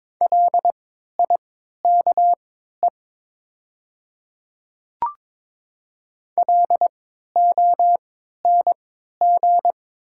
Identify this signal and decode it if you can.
Morse code, heard as a keyed tone, about 22 wpm, spaced as for 13 wpm, 700 Hz. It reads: LIKE LONG
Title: 3x Character Spacing - Top 100 Words - 22wpm